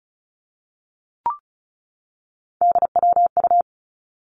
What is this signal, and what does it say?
Morse code, heard as a keyed tone, about 36 wpm, 700 Hz. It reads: DWU
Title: Three Challenging Letter-Number Combinations 35wpm